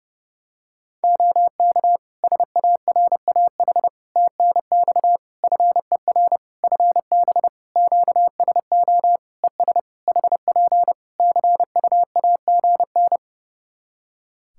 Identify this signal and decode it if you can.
Morse code, heard as a keyed tone, about 30 wpm, 700 Hz. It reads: OK SARAH TNX FER FB QSO ES HP CUAGN
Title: Practice QSO - Exchange 4 - Ending - 30wpm